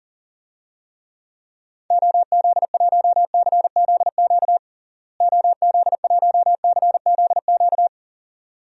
Morse code, heard as a keyed tone, about 40 wpm, 700 Hz. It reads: OZ1CZQ OZ1CZQ
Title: DXCC-100 Call Signs - 6 Characters - Repeated 2x - 40wpm